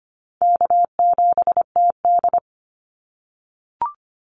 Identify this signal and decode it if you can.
Morse code, heard as a keyed tone, about 25 wpm, 700 Hz. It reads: K7TD
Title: Easy Call Signs - 25wpm